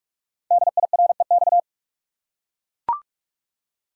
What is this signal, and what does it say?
Morse code, heard as a keyed tone, about 44 wpm, 700 Hz. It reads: DIREX
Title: Code Groups - 5 Letters - 45wpm